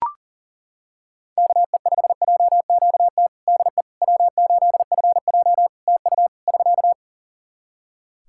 Morse code, heard as a keyed tone, about 37 wpm, 700 Hz. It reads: KE5JQT DE W8FJ TU <SK>
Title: Practice QSO - Exchange 4 - Ending - 40wpm